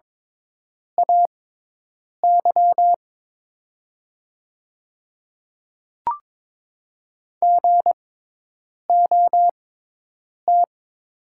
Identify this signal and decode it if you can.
Morse code, heard as a keyed tone, about 22 wpm, spaced as for 8 wpm, 700 Hz. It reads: AY GOT
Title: 6x Character Spacing - Top 300 Words - 22wpm